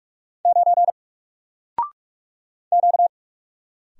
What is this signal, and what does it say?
Morse code, heard as a keyed tone, about 44 wpm, 700 Hz. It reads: 9 Q
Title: Single Character (Letter:Number:Punctuation:Prosign) Alt 45wpm